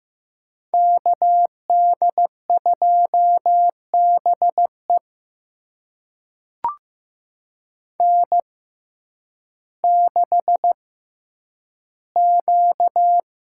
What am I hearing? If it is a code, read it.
Morse code, heard as a keyed tone, about 15 wpm, spaced as for 9 wpm, 700 Hz. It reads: KD2BE N6Q
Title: USA Call Signs - 5 Characters - 6x Character Spacing - 15wpm